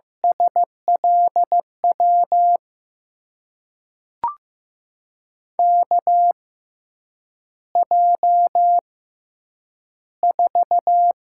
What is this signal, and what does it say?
Morse code, heard as a keyed tone, about 15 wpm, spaced as for 8 wpm, 700 Hz. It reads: SLW KJ4
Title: USA Call Signs - 6 Characters - 6x Character Spacing - 15wpm